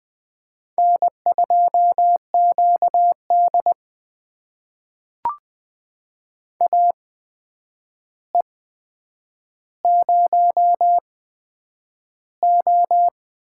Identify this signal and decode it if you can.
Morse code, heard as a keyed tone, about 20 wpm, spaced as for 9 wpm, 700 Hz. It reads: N2QD AE0O
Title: USA Call Signs - 4 Characters - 8x Character Spacing - 20wpm